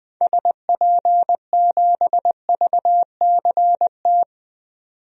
Morse code, heard as a keyed tone, about 20 wpm, 700 Hz. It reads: SP7VCT
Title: DXCC-100 Call Signs - 6 Characters - 20wpm